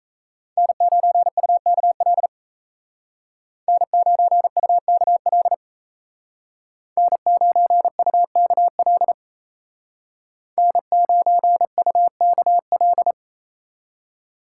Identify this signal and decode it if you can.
Morse code, heard as a keyed tone, about 33 wpm, 700 Hz. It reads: N9UKL N9UKL N9UKL N9UKL
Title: Speed Racing - USA 1x3 Call Signs 28wpm